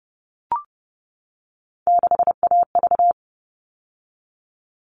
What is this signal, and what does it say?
Morse code, heard as a keyed tone, about 30 wpm, 700 Hz. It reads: 6AV
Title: All Cut Numbers - 3-Digit 30wpm